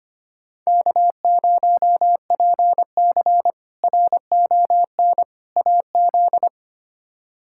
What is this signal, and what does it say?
Morse code, heard as a keyed tone, about 25 wpm, 700 Hz. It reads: K0PC RON AZ
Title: K1USN SST QSO Practice - 25wpm